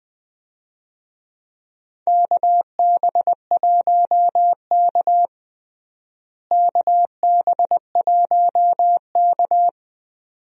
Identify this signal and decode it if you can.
Morse code, heard as a keyed tone, about 20 wpm, 700 Hz. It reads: KB1K KB1K
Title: USA Call Signs - 4 Characters - Repeated 2x - 20wpm